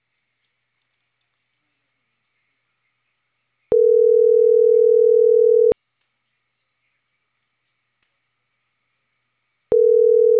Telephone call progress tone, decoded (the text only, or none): ringback tone